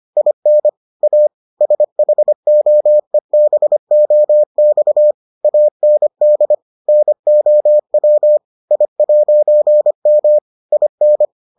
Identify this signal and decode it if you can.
Morse code, heard as a keyed tone, about 25 wpm, 600 Hz. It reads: IN A SHOEBOX AND NOW I'M IN